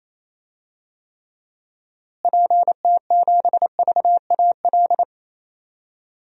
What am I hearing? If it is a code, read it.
Morse code, heard as a keyed tone, about 28 wpm, 700 Hz. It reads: PT7VAL
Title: DXCC-100 Call Signs - 6 Characters - Repeated 3x - 28wpm